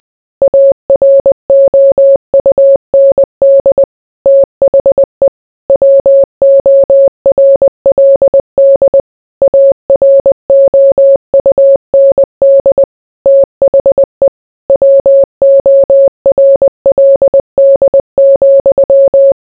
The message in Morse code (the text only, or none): AROUND THE WORLD AROUND THE WORLD,